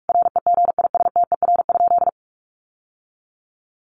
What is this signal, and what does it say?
Morse code, heard as a keyed tone, about 44 wpm, 700 Hz. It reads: REGISTER?
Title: Sentences from Top 1200 Words 45wpm